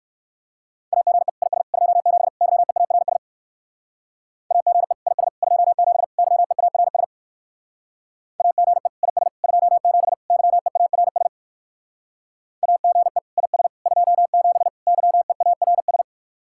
Morse code, heard as a keyed tone, about 59 wpm, 700 Hz. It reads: AGE IS 27 YEARS AGE IS 27 YEARS AGE IS 27 YEARS AGE IS 27 YEARS